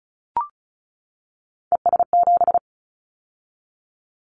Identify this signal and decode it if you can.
Morse code, heard as a keyed tone, about 35 wpm, 700 Hz. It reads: ES7
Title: Callsign Prefixes for Top 50 DXCC - 35wpm